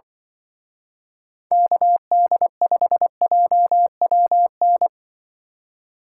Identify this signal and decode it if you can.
Morse code, heard as a keyed tone, about 24 wpm, 700 Hz. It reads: KD5JWN
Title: Speed Racing - USA 2x3 Call Signs 20wpm